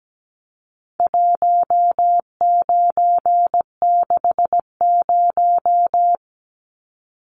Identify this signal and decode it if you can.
Morse code, heard as a keyed tone, about 17 wpm, 700 Hz. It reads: 1960
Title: Numbers - 4-Digit 17wpm